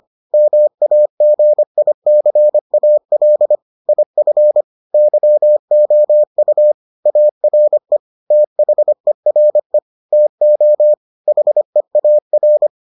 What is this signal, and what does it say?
Morse code, heard as a keyed tone, about 25 wpm, 600 Hz. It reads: MAGICAL IF YOU ARE THERE TO HEAR